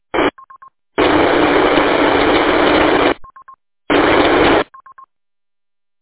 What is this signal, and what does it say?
There is no code.